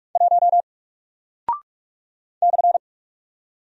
Morse code, heard as a keyed tone, about 44 wpm, 700 Hz. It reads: 1 /
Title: Single Character (Letter:Number:Punctuation:Prosign) Alt 45wpm